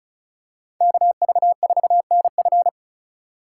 Morse code, heard as a keyed tone, about 34 wpm, 700 Hz. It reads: KV4NF
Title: USA Call Signs - 5 Characters - 3x Character Spacing - 35wpm